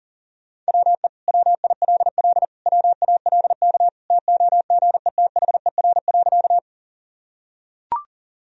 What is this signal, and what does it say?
Morse code, heard as a keyed tone, about 37 wpm, 700 Hz. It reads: WE WILL WALK TOGETHER.
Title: Sentences from Top 1000 Words - Review Part 1 - 40wpm